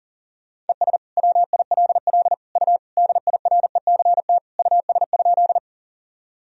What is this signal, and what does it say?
Morse code, heard as a keyed tone, about 40 wpm, 700 Hz. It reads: ES WILL U DIRECT US?
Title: Sentences from Top 400 Words and Abbreviations 40wpm